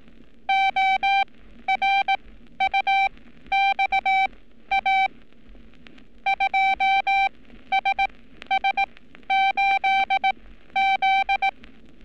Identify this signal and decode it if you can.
Morse code, heard as a keyed tone, about 18 wpm, spaced as for 12 wpm, 780 Hz. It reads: ORUXA 2SS8Z
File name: training / NAVPERS 11224-11235 / NAVPERS 11227 RA RB